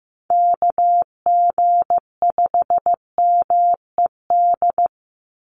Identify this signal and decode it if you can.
Morse code, heard as a keyed tone, about 15 wpm, 700 Hz. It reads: KG5MED
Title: USA Call Signs 15wpm